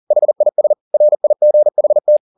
Morse code, heard as a keyed tone, about 40 wpm, 600 Hz. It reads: HIS RIGHT
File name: the old man and the sea 2 40wpm